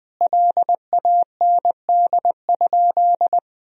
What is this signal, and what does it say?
Morse code, heard as a keyed tone, about 20 wpm, 700 Hz. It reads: LAND?
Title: Sentences from Top 1000 Words - Review Part 1 - 20wpm